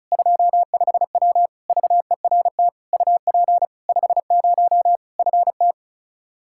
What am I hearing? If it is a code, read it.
Morse code, heard as a keyed tone, about 36 wpm, 700 Hz. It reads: 25W VERT UP 50 FT
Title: Practice QSO - Exchange 2 - Info - 35wpm